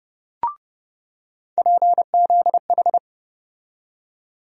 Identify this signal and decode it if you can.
Morse code, heard as a keyed tone, about 29 wpm, 700 Hz. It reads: PZH
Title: USA Call Sign Postfixes - 3 Letters 30wpm